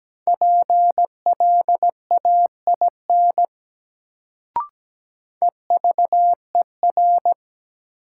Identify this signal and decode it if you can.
Morse code, heard as a keyed tone, about 17 wpm, 700 Hz. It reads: PLAIN EVER
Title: Sentences from Top 400 Words - New Word Drill - 17wpm